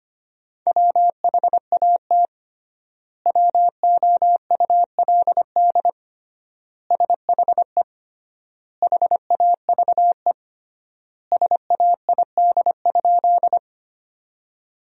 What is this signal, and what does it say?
Morse code, heard as a keyed tone, about 25 wpm, 700 Hz. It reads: WHAT WOULD SHE HAVE SAID?
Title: 3x Word-Spacing - Sentences from Top 100 Words - 25wpm